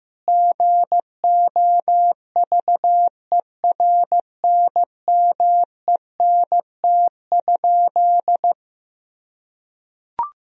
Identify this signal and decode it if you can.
Morse code, heard as a keyed tone, about 15 wpm, 700 Hz. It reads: GOVERNMENT?
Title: Sentences from Top 1000 Words - Review Part 3 - 15wpm